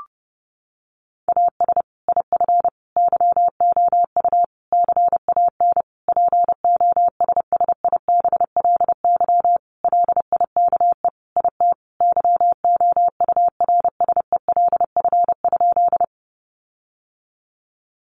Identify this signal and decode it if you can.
Morse code, heard as a keyed tone, about 30 wpm, 700 Hz. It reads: AS IF YOU CAN POSSIBLY LIKE IT YOURSELF?
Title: Sentences from Top 1300 Words 30wpm